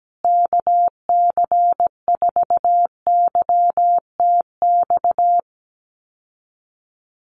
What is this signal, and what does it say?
Morse code, heard as a keyed tone, about 17 wpm, 700 Hz. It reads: KC4YTX
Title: USA Call Signs Repeated 3x 17wpm